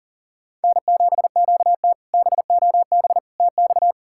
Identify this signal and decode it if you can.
Morse code, heard as a keyed tone, about 40 wpm, 690 Hz. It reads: N7QT BOB TX
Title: K1USN SST QSO Practice - 40wpm